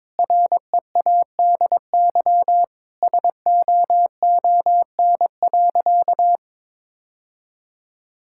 Morse code, heard as a keyed tone, about 21 wpm, 700 Hz. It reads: READY SOON.
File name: practice / 22wpm / Sentences with 2-Word Phrases - 22wpm